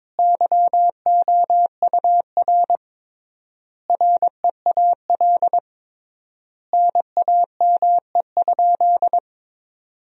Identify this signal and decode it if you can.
Morse code, heard as a keyed tone, about 22 wpm, 700 Hz. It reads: YOUR REAL NAME?
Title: 3x Word-Spacing - Sentences from Top 300 Words - 22wpm